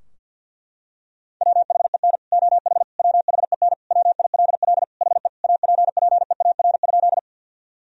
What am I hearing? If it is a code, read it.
Morse code, heard as a keyed tone, about 51 wpm, 700 Hz. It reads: WHEN OH WHEN WILL HE APPEAR?
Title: Sentences from Top 1000 Words - Review Part 2 - 50wpm